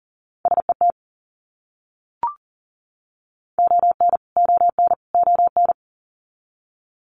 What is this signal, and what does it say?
Morse code, heard as a keyed tone, about 39 wpm, 700 Hz. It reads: SET ON ON ON
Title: Top 300 Words Repeated 40wpm